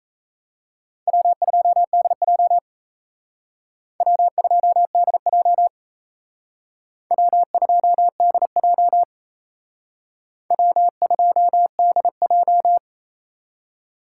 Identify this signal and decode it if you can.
Morse code, heard as a keyed tone, about 36 wpm, 700 Hz. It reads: W2DJ W2DJ W2DJ W2DJ